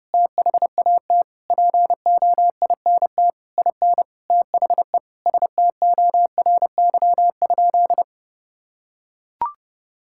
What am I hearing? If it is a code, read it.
Morse code, heard as a keyed tone, about 30 wpm, 700 Hz. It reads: THAT POINT IN THE STORY?
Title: Sentences from Top 1000 Words - Review Part 5 - 30wpm